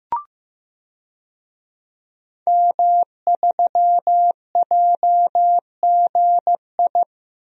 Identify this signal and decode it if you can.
Morse code, heard as a keyed tone, about 16 wpm, 700 Hz. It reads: M3JGI
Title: DXCC-100 Call Signs - 5 Characters - Repeated 2x - 15wpm